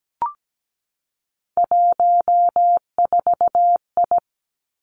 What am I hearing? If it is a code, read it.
Morse code, heard as a keyed tone, about 17 wpm, 700 Hz. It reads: 14I
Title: Three Challenging Letter-Number Combinations 17wpm